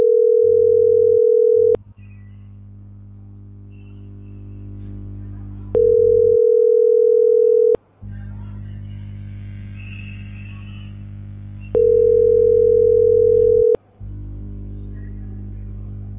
Ringback tone (North America), 440 + 480 Hz.